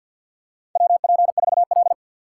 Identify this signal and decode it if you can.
Morse code, heard as a keyed tone, about 49 wpm, 700 Hz. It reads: WP4L
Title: USA Call Signs - 3x Character Spacing - 50wpm